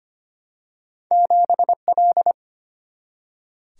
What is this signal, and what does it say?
Morse code, heard as a keyed tone, about 25 wpm, 700 Hz. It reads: 7L
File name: Two Letter-Number Combinations - 25wpm